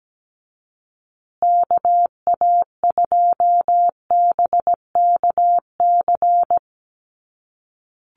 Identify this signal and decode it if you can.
Morse code, heard as a keyed tone, about 17 wpm, 700 Hz. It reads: KA2BKC
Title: USA Call Signs Repeated 3x 17wpm